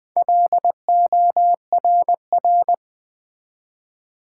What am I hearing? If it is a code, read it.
Morse code, heard as a keyed tone, about 20 wpm, 700 Hz. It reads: LORR
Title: Build A Word - 1 Letter At A Time - 3 to 5-Letter Words - 20wpm